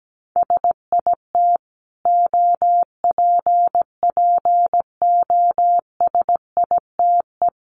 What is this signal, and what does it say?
Morse code, heard as a keyed tone, about 17 wpm, 700 Hz. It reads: SIT OPPOSITE
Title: Sentences from Top 1300 Words 17wpm